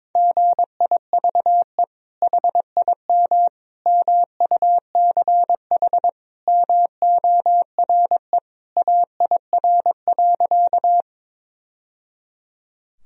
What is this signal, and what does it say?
Morse code, heard as a keyed tone, about 22 wpm, 700 Hz. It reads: GIVE HIM MUCH MORE AIR.
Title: Sentences from Top 1000 Words - Review Part 1 - 22wpm